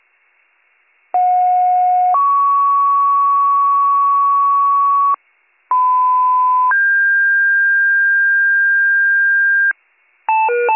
Two-tone page, heard as a one-second tone, then a three-second tone, about 730 then 1120 Hz, about 1010 then 1600 Hz. It is heard from 1.1 to 9.7 s.